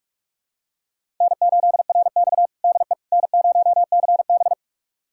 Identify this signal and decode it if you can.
Morse code, heard as a keyed tone, about 41 wpm, 700 Hz. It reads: N8RX DE N0CB
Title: USA Callsign DE Callsign - 4 Characters - 45wpm